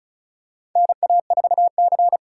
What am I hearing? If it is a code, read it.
Morse code, heard as a keyed tone, about 36 wpm, 700 Hz. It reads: NA4C
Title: K1USN SST QSO Practice - 35wpm